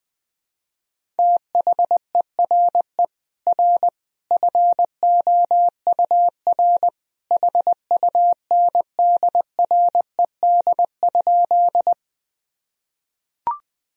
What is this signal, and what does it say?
Morse code, heard as a keyed tone, about 20 wpm, 700 Hz. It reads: THERE R FOUR HUNDRED?
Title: Sentences from Top 400 Words and Abbreviations 20wpm